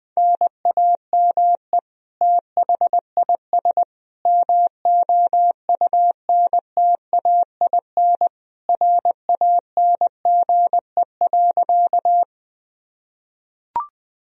Morse code, heard as a keyed tone, about 21 wpm, 700 Hz. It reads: NAME THIS MOUNTAIN RANGE.
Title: Sentences from Top 1000 Words - Review Part 4 - 20wpm